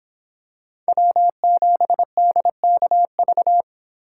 Morse code, heard as a keyed tone, about 25 wpm, 700 Hz. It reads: W7DKV